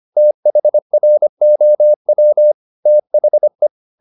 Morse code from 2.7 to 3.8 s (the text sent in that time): THE